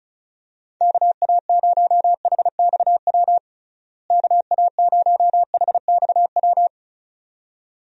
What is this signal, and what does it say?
Morse code, heard as a keyed tone, about 34 wpm, 700 Hz. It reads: KA0HXW KA0HXW
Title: USA Call Signs - 6 Characters - Repeated 2x - 35wpm